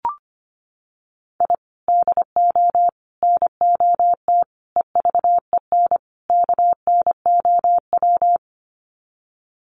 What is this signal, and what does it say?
Morse code, heard as a keyed tone, about 25 wpm, 700 Hz. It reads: I DO NOT EVEN KNOW
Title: Sets of 5 Words 25wpm